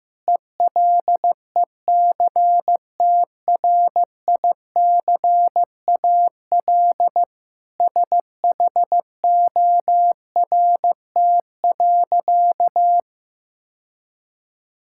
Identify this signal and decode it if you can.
Morse code, heard as a keyed tone, about 15 wpm, 700 Hz. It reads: ELECTRICAL SHORT.
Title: Sentences from Top 1000 Words - Review Part 6 - 15wpm